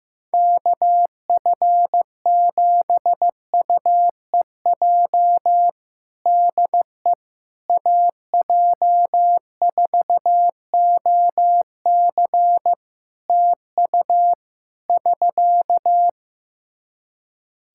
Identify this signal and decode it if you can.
Morse code, heard as a keyed tone, about 15 wpm, 700 Hz. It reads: KF7UEJ DE AJ4OC TU <SK>
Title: Practice QSO - Exchange 4 - Ending - 15wpm